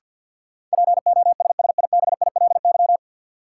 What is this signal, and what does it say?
Morse code, heard as a keyed tone, about 52 wpm, 700 Hz. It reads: POSSIBILY
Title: Sentences from Top 1300 Words - New Word Drill - 50wpm